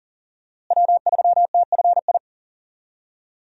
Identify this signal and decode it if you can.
Morse code, heard as a keyed tone, about 40 wpm, 700 Hz. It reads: W3TFI
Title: USA Call Signs - 5 Characters - Repeated 3x - 40wpm